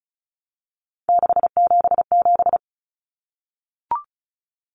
Morse code, heard as a keyed tone, about 34 wpm, 700 Hz. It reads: 677
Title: Numbers - 3-Digit 35wpm